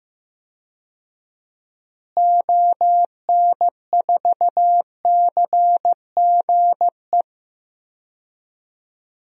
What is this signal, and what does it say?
Morse code, heard as a keyed tone, about 16 wpm, 700 Hz. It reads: ON4CGE
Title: DXCC-100 Call Signs - 6 Characters - Repeated 2x - 15wpm